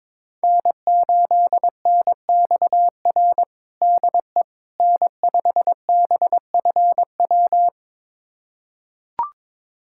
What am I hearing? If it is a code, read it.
Morse code, heard as a keyed tone, about 22 wpm, 700 Hz. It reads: N8NXR DE N5BFW
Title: USA Callsign DE Callsign - 5 Characters - 22wpm